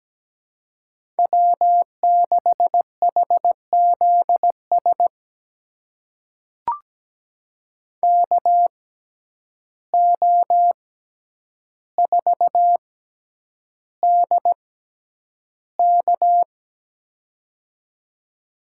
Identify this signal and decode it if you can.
Morse code, heard as a keyed tone, about 17 wpm, spaced as for 9 wpm, 700 Hz. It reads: W6HZS KO4DK
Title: USA Call Signs - 5 Characters - 6x Character Spacing - 17wpm